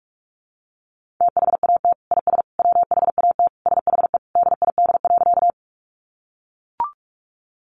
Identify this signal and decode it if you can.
Morse code, heard as a keyed tone, about 44 wpm, 700 Hz. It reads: THAT IS WHAT SHE DID.
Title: Sentences from Top 100 Words 45wpm